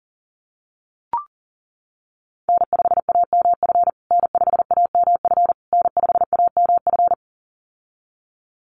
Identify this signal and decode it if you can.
Morse code, heard as a keyed tone, about 40 wpm, 700 Hz. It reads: N5AMF N5AMF N5AMF
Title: USA Call Signs Repeated 3x 40wpm